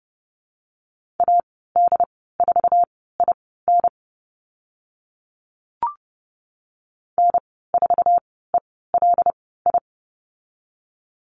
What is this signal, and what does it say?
Morse code, heard as a keyed tone, about 30 wpm, spaced as for 17 wpm, 700 Hz. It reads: AD4IN N4ELI
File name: Lesson 19g - Letter w - Call Signs 30wpm